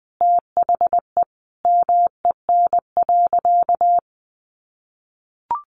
Morse code, heard as a keyed tone, about 20 wpm, 700 Hz. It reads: THE MEN.
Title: Sentences from Top 1400 Words 20wpm